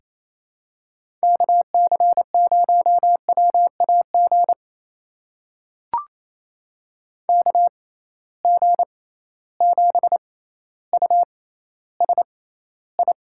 Morse code, heard as a keyed tone, about 28 wpm, spaced as for 14 wpm, 700 Hz. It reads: KC0WAG KG7USI
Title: USA Call Signs - 6 Characters - 6x Character Spacing - 28wpm